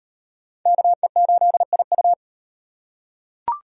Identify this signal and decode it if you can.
Morse code, heard as a keyed tone, about 38 wpm, 700 Hz. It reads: KE8IU